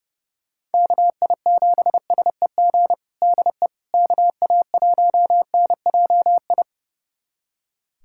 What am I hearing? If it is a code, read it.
Morse code, heard as a keyed tone, about 31 wpm, 700 Hz. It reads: KI7SEG DE KA1NJI
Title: USA Callsign DE Callsign - 6 Characters - 30wpm